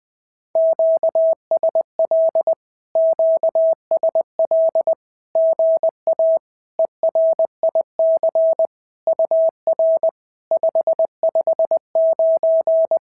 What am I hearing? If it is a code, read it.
Morse code, heard as a keyed tone, about 20 wpm, 660 Hz. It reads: QSL QSL GA ERIC UR 559
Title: SOTA QSO Practice 20wpm